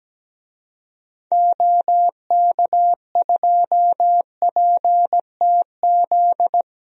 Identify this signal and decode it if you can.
Morse code, heard as a keyed tone, about 17 wpm, 700 Hz. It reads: OK2PTZ
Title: DXCC-100 Call Signs - 6 Characters - 17wpm